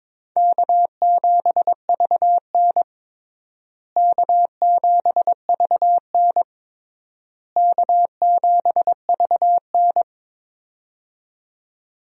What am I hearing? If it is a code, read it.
Morse code, heard as a keyed tone, about 22 wpm, 700 Hz. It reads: K7VN K7VN K7VN
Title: USA Call Signs - 4 Characters - Repeated 3x - 22wpm